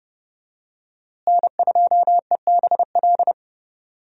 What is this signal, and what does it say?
Morse code, heard as a keyed tone, about 30 wpm, 700 Hz. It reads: N2EBL
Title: USA Call Signs - 5 Characters - Repeated 2x - 30wpm